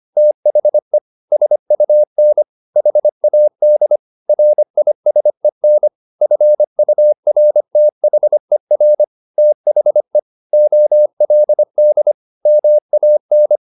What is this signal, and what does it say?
Morse code, heard as a keyed tone, about 25 wpm, 600 Hz. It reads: THE SUN HAD RISEN FURTHER THE OLD MAN
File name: the old man and the sea part 1 25wpm